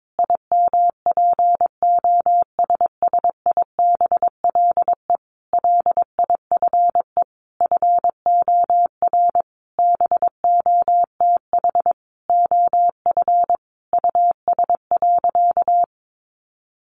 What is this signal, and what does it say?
Morse code, heard as a keyed tone, about 22 wpm, 700 Hz. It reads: IMPOSSIBLE LIFE FOR BOTH OF US.
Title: Sentences from Top 1100 Words 22wpm